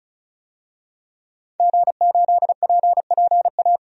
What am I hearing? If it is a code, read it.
Morse code, heard as a keyed tone, about 34 wpm, 700 Hz. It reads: G8PPA
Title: DXCC-100 Call Signs - 5 Characters - 35wpm